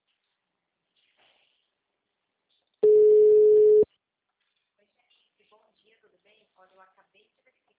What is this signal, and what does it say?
Ringback tone (Europe, 425 Hz), with speech.